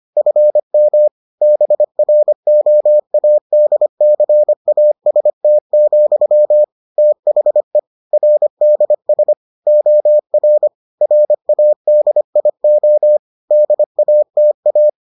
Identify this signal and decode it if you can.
Morse code, heard as a keyed tone, about 25 wpm, 600 Hz. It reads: FM BROADCAST, THE RDS OR RADIO DATA